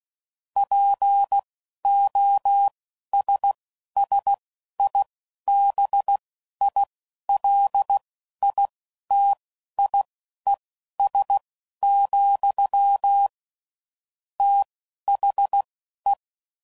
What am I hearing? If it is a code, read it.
Morse code, heard as a keyed tone, about 16 wpm, spaced as for 11 wpm, 800 Hz. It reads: POSSIBILITIES, THE
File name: CW Code Groups @10WPM/Technical Subjects